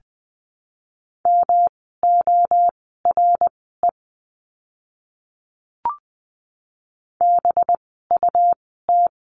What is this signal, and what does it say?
Morse code, heard as a keyed tone, about 20 wpm, spaced as for 15 wpm, 700 Hz. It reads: MORE BUT
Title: 192 Lesson 31f - Letter v - Words 20wpm